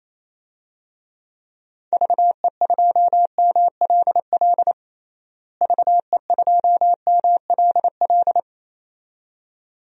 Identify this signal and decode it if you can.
Morse code, heard as a keyed tone, about 28 wpm, 700 Hz. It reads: VE2MLL VE2MLL
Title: DXCC-100 Call Signs - 6 Characters - Repeated 2x - 28wpm